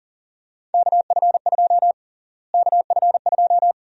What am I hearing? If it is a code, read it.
Morse code, heard as a keyed tone, about 40 wpm, 700 Hz. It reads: KF2 KF2
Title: USA Call Sign Prefixes - Repeated 2x - 40wpm